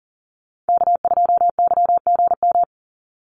Morse code, heard as a keyed tone, about 41 wpm, 700 Hz. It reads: K2YGM
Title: Call Signs - 40wpm